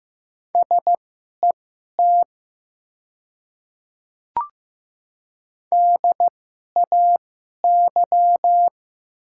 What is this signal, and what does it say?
Morse code, heard as a keyed tone, about 15 wpm, spaced as for 11 wpm, 700 Hz. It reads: SET DAY
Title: 2x Character Spacing - Top 500 Words - 15wpm